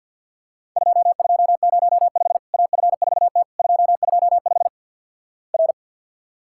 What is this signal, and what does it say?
Morse code, heard as a keyed tone, about 50 wpm, 700 Hz. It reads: 2205 AF4T 225 R
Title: Worked All Europe DX Contest - QTC Exchange - 50wpm